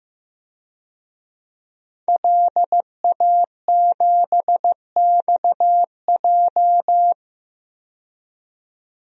Morse code, heard as a keyed tone, about 15 wpm, 700 Hz. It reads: LA7XJ
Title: DXCC-100 Call Signs - 5 Characters - 15wpm